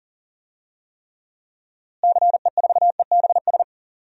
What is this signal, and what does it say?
Morse code, heard as a keyed tone, about 40 wpm, 700 Hz. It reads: CE4EBS
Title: DXCC-100 Call Signs - 6 Characters - Repeated 2x - 40wpm